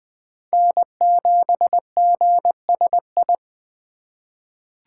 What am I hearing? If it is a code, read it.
Morse code, heard as a keyed tone, about 20 wpm, 700 Hz. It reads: N7GSI
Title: USA Call Signs - 5 Characters - 4x Character Spacing - 20wpm